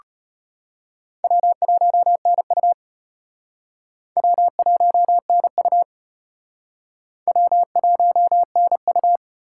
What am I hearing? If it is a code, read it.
Morse code, heard as a keyed tone, about 32 wpm, 700 Hz. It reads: W1NU W1NU W1NU